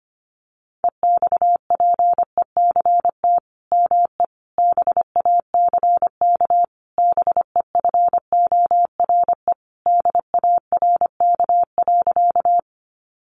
Morse code, heard as a keyed tone, about 25 wpm, 700 Hz. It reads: EXPECT ME BACK BEFORE DARK.
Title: Sentences from Top 1100 Words 25wpm